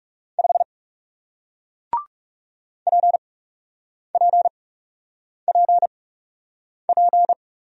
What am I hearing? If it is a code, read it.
Morse code, heard as a keyed tone, about 37 wpm, spaced as for 8 wpm, 700 Hz. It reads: 5 PPPP
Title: Speed-Racing Single Letter-Number 30wpm